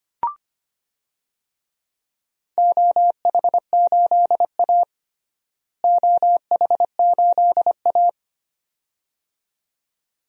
Morse code, heard as a keyed tone, about 25 wpm, 700 Hz. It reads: OH8A OH8A
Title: DXCC-100 Call Signs - 4 Characters - Repeated 2x - 25wpm